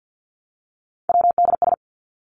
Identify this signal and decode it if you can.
Morse code, heard as a keyed tone, about 50 wpm, 700 Hz. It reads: WDS
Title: Abbreviations 50wpm